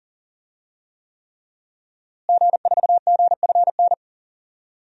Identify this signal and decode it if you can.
Morse code, heard as a keyed tone, about 40 wpm, 700 Hz. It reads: G4GFN
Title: DXCC-100 Call Signs - 5 Characters - Repeated 3x - 40wpm